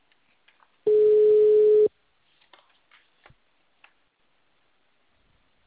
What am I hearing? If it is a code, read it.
Ringback tone (Europe, 425 Hz).